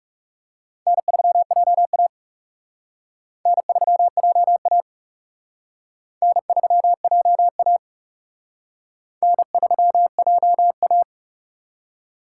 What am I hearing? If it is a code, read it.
Morse code, heard as a keyed tone, about 36 wpm, 700 Hz. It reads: N3JA N3JA N3JA N3JA
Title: Speed Racing - USA 1x2 Call Signs 30wpm